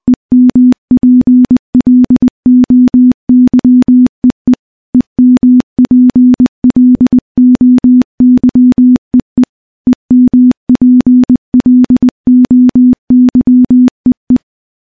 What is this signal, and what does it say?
Morse code, heard as a keyed tone, about 20 wpm, 250 Hz. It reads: EMPLOYEE EMPLOYEE EMPLOYEE
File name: morse / morse-training-20WPM